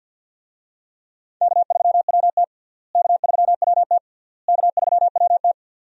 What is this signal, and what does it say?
Morse code, heard as a keyed tone, about 52 wpm, 700 Hz. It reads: K3WT K3WT K3WT
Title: USA Call Signs - 4 Characters - Repeated 3x - 50wpm